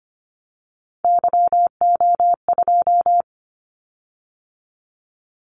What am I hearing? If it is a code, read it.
Morse code, heard as a keyed tone, about 25 wpm, 700 Hz. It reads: YO2
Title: Callsign Prefixes for Top 25 DXCC - 25wpm